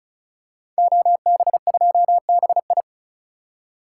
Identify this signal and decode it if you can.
Morse code, heard as a keyed tone, about 36 wpm, 700 Hz. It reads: OB2BI
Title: Code Groups - 5 Letter:Numbers - 35wpm